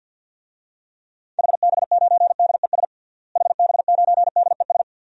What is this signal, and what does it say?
Morse code, heard as a keyed tone, about 51 wpm, 700 Hz. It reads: HB9DES HB9DES
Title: DXCC-100 Call Signs - 6 Characters - Repeated 2x - 50wpm